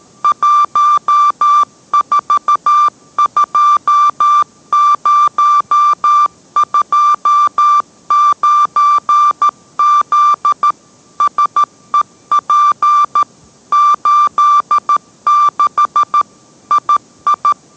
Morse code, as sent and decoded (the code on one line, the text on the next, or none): .---- ....- ..--- ----- ..--- ----. --.. / ... .--. / ---.. -.... / .. ..
142029Z SEP 86 II